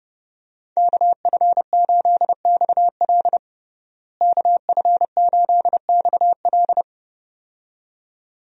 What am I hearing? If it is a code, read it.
Morse code, heard as a keyed tone, about 30 wpm, 700 Hz. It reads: KF8XL KF8XL
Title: USA Call Signs - 5 Characters - Repeated 2x - 30wpm